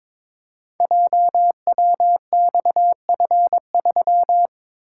Morse code, heard as a keyed tone, about 22 wpm, 700 Hz. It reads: JWXF3
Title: Code Groups - 5 Letter:Numbers - 22wpm